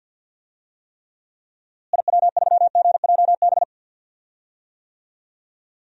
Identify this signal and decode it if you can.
Morse code, heard as a keyed tone, about 50 wpm, 700 Hz. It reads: IW3GJB
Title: DXCC-100 Call Signs - 6 Characters - Repeated 2x - 50wpm